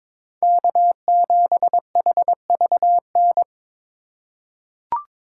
Morse code, heard as a keyed tone, about 22 wpm, 700 Hz. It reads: K7HVN
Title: USA Call Signs - 2x Character Spacing - 22wpm